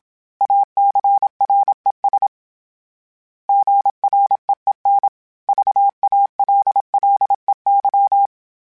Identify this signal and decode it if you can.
Morse code, heard as a keyed tone, about 27 wpm, 800 Hz. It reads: ACRES GREEN VALLEY
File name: fromSMs / FromSMs mp3s / Cities@25